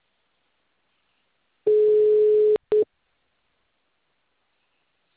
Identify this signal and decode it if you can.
Ringback tone (Europe, 425 Hz).